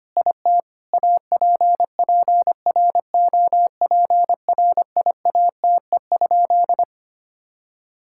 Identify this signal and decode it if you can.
Morse code, heard as a keyed tone, about 25 wpm, 700 Hz. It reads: IT APPROPRIATE?
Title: Sentences from Top 1000 Words - Review Part 5 - 25wpm